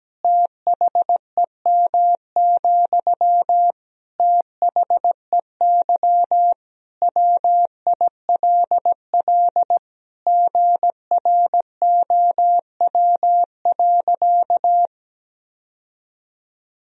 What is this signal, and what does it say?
Morse code, heard as a keyed tone, about 17 wpm, 700 Hz. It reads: THEM, THEY WILL GROW.
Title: Sentences from Top 1000 Words - Review Part 1 - 17wpm